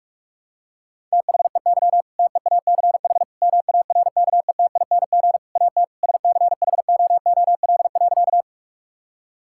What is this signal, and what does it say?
Morse code, heard as a keyed tone, about 46 wpm, 700 Hz. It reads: THEY TEACH MARKETING AT SCHOOL.
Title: Sentences from Top 1000 Words - Review Part 5 - 45wpm